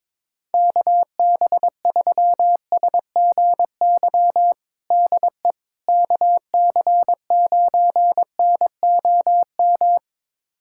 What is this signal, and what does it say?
Morse code, heard as a keyed tone, about 22 wpm, 700 Hz. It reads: KB3SGY DE KC9NOM
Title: USA Callsign DE Callsign - 6 Characters - 22wpm